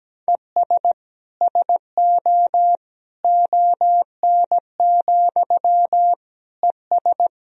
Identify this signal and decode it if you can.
Morse code, heard as a keyed tone, about 17 wpm, 700 Hz. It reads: ES SO ON, ES
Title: Sentences from Top 100 Words and Abbreviations 17wpm